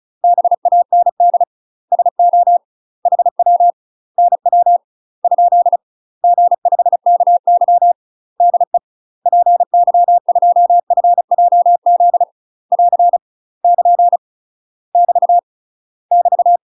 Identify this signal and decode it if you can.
Morse code, heard as a keyed tone, about 36 wpm, 700 Hz. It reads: BAND SO HW NW ? G5KY DE PY2FJZ <AR> <KN> <BT> <BT>
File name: UK MASTER QSOs TRAINING / 35WPM / UK MASTER QSOs 0022